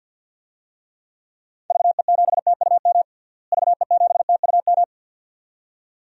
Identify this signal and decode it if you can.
Morse code, heard as a keyed tone, about 46 wpm, 700 Hz. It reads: VE7TUM VE7TUM
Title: DXCC-100 Call Signs - 6 Characters - Repeated 2x - 50wpm